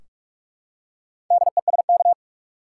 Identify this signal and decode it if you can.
Morse code, heard as a keyed tone, about 43 wpm, 700 Hz. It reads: DESK
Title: Four Letter Words - 45wpm